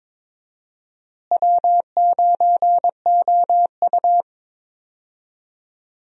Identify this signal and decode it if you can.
Morse code, heard as a keyed tone, about 22 wpm, 700 Hz. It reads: W9OU
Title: USA Call Signs - 4 Characters - Repeated 2x - 22wpm